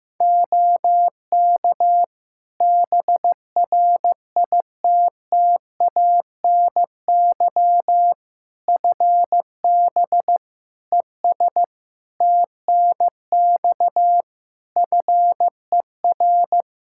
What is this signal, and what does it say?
Morse code, heard as a keyed tone, about 15 wpm, 700 Hz. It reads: OK BRITTANY FB ES TNX FER
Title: K1USN SST QSO Practice 15wpm